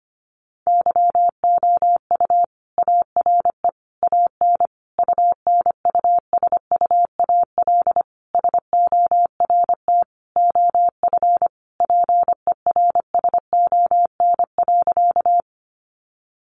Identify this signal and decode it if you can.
Morse code, heard as a keyed tone, about 25 wpm, 700 Hz. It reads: YOU ARE AN UNUSUAL SORT OF PERSON.
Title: Sentences from Top 1400 Words 25wpm